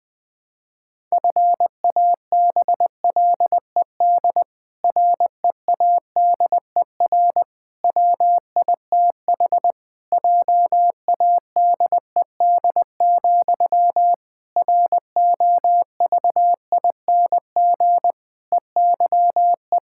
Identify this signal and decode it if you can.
Morse code, heard as a keyed tone, about 20 wpm, 700 Hz. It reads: FABLED READER WITH JADED, ROVING EYE
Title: Pangram Challenge v2 - 20wpm